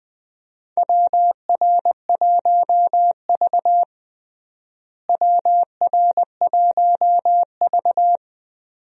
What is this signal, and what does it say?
Morse code, heard as a keyed tone, about 20 wpm, 700 Hz. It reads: WR1V WR1V
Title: USA Call Signs - 4 Characters - Repeated 2x - 20wpm